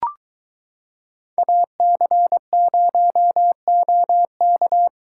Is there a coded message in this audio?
Yes: Morse code from 1.4 to 4.9 s, heard as a keyed tone, about 23 wpm, 700 Hz. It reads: AC0OK